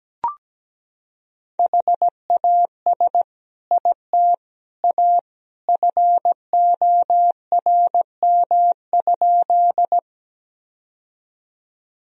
Morse code, heard as a keyed tone, about 17 wpm, 700 Hz. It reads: HAS IT A FORM?
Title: Sentences from Top 1000 Words - Review Part 1 - 17wpm